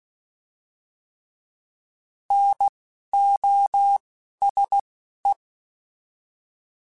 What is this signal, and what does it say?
Morse code, heard as a keyed tone, about 16 wpm, spaced as for 11 wpm, 800 Hz. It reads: NOSE